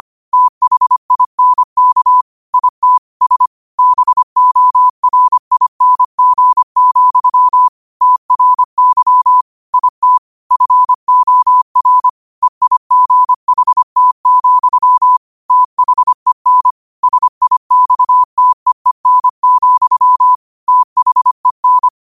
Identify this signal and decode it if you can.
Morse code, heard as a keyed tone, about 25 wpm, 1000 Hz. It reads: THINK IT S BORING, TRY IT FOR EIGHT, THEN SIXTEEN, THEN
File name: qotdfiles / 25wpm